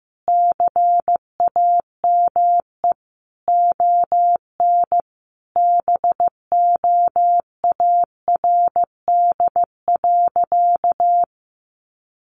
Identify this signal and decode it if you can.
Morse code, heard as a keyed tone, about 15 wpm, 700 Hz. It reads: CAME ON BOARD.
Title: Sentences from Top 1500 Words 15wpm